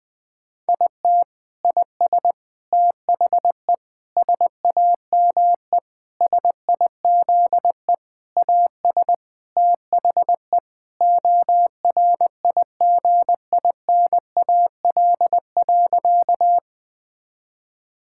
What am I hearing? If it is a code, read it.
Morse code, heard as a keyed tone, about 20 wpm, 700 Hz. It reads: IT IS THE SAME SIZE AS THE ORIGINAL.
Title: Sentences from Top 1000 Words - Review Part 6 - 20wpm